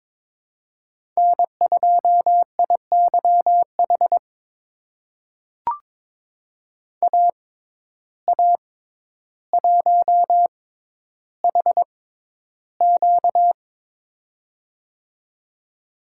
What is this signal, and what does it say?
Morse code, heard as a keyed tone, about 21 wpm, spaced as for 11 wpm, 700 Hz. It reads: N2IYH AA1HQ
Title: USA Call Signs - 5 Characters - 6x Character Spacing - 22wpm